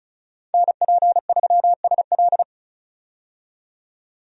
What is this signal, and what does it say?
Morse code, heard as a keyed tone, about 36 wpm, 700 Hz. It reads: NP3SL
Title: USA Call Signs - 5 Characters - 35wpm